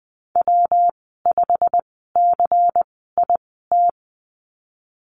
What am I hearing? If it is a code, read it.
Morse code, heard as a keyed tone, about 19 wpm, spaced as for 14 wpm, 700 Hz. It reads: W5CIT
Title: 128 Lesson 21g - Number 6 - Call Signs 20wpm